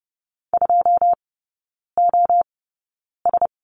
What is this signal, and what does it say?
Morse code, heard as a keyed tone, about 30 wpm, spaced as for 9 wpm, 700 Hz. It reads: 2OS
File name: 081 Lesson 14e - Number 2 - Three Characters 30wpm